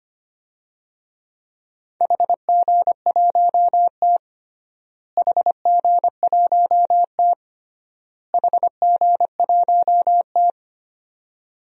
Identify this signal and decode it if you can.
Morse code, heard as a keyed tone, about 25 wpm, 700 Hz. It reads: HG1T HG1T HG1T
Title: DXCC-100 Call Signs - 4 Characters - Repeated 3x - 25wpm